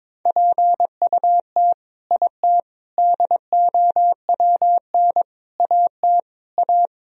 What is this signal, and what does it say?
Morse code, heard as a keyed tone, about 21 wpm, 700 Hz. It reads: PUT IT DOWN AT A
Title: Sentences from Top 1000 Words - Review Part 3 - 22wpm